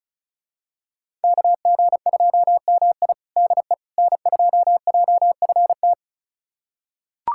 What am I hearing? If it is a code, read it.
Morse code, heard as a keyed tone, about 36 wpm, 700 Hz. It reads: KG2MI DE N2JFT
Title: USA Callsign DE Callsign 35wpm